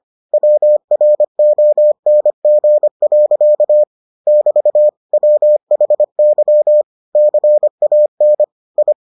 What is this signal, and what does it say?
Morse code, heard as a keyed tone, about 25 wpm, 600 Hz. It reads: WRONG. <BT> WHY CAN I